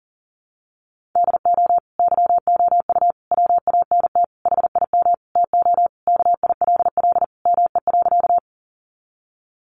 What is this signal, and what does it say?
Morse code, heard as a keyed tone, about 40 wpm, 700 Hz. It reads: DO YOU WANT HIM TO KILL ME.
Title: Sentences from Top 1400 Words 40wpm